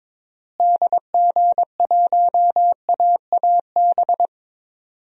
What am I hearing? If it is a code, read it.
Morse code, heard as a keyed tone, about 22 wpm, 700 Hz. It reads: DG1AAB
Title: DXCC-100 Call Signs - 6 Characters - Repeated 2x - 22wpm